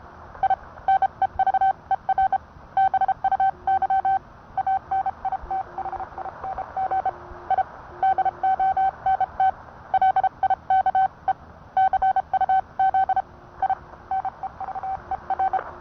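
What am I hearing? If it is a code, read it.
Morse code, heard as a keyed tone, about 30 wpm, 770 Hz. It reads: I NEVER BUY ANITHING I DONT LIKE CUZ I NEVER
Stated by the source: Morse code sent by N1EA on a Vibroplex key.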